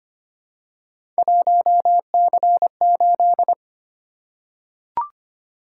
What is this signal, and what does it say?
Morse code, heard as a keyed tone, about 25 wpm, 700 Hz. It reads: 1C8
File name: Three Letter-Number Combinations - 25wpm